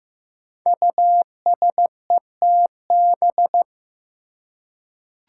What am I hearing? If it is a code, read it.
Morse code, heard as a keyed tone, about 15 wpm, 700 Hz. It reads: USETB
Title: Five Challenging Letter-Number Combinations 15wpm